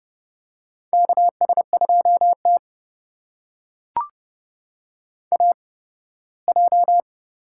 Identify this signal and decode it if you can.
Morse code, heard as a keyed tone, about 32 wpm, spaced as for 18 wpm, 700 Hz. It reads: KS2T AJ
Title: USA Call Signs - 4 Characters - 8x Character Spacing - 30wpm